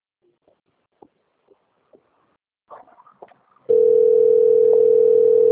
Ringback tone (North America), 440 + 480 Hz.
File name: out- - - - - .23490